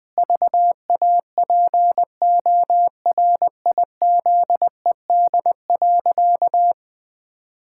Morse code, heard as a keyed tone, about 20 wpm, 700 Hz. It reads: VAPORIZED.
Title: Pangram Challenge v2 - 20wpm